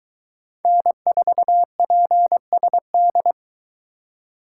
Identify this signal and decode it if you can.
Morse code, heard as a keyed tone, about 22 wpm, 700 Hz. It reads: N4PSD